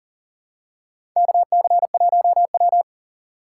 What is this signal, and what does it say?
Morse code, heard as a keyed tone, about 40 wpm, 700 Hz. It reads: KC1W
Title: USA Call Signs - 4 Characters - 6x Character Spacing - 40wpm